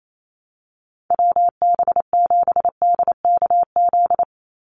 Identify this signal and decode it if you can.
Morse code, heard as a keyed tone, about 28 wpm, 700 Hz. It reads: WB7DKZ
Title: Call Signs Repeated 3x - 28wpm